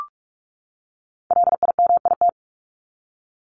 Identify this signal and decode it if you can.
Morse code, heard as a keyed tone, about 44 wpm, 700 Hz. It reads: LIMIT
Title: ED Suffix - Top 100 Words with Base Words 45wpm